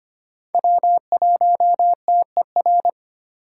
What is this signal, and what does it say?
Morse code, heard as a keyed tone, about 25 wpm, 700 Hz. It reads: W1TER
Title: K1USN SST QSO Practice - 25wpm